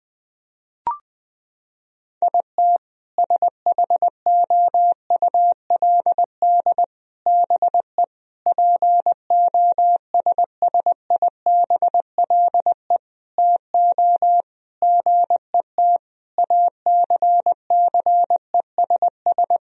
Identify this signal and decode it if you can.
Morse code, heard as a keyed tone, about 20 wpm, 700 Hz. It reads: IT SHOULD BE POSSIBLE TO GET ACCESS
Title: Sentences from Top 1000 Words - Review Part 4 - 20wpm